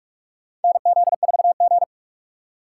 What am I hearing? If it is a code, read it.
Morse code, heard as a keyed tone, about 47 wpm, 700 Hz. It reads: NZ4G
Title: USA Call Signs - 2x Character Spacing - 45wpm